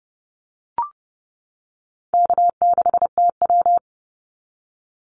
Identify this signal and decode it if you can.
Morse code, heard as a keyed tone, about 30 wpm, 700 Hz. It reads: K6TW
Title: NA SOTA Activator Callsigns 30wpm